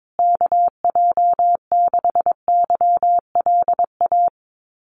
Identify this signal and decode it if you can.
Morse code, heard as a keyed tone, about 22 wpm, 700 Hz. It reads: KJ6YLA
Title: USA Call Signs Repeated 2x 22wpm